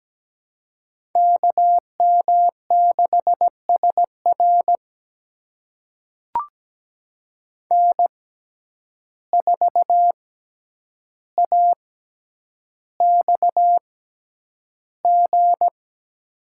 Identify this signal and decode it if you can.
Morse code, heard as a keyed tone, about 17 wpm, spaced as for 9 wpm, 700 Hz. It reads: KM6SR N4AXG
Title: USA Call Signs - 5 Characters - 6x Character Spacing - 17wpm